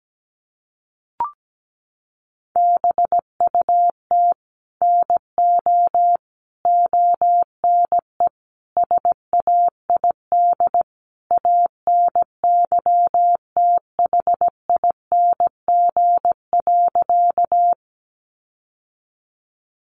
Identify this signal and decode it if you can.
Morse code, heard as a keyed tone, about 17 wpm, 700 Hz. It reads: BUT NO ONE SAID ANYTHING.
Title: Sentences from Top 1000 Words 17wpm